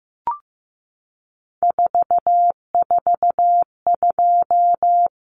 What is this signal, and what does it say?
Morse code, heard as a keyed tone, about 15 wpm, 700 Hz. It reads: 442
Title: Numbers - 3-Digit 15wpm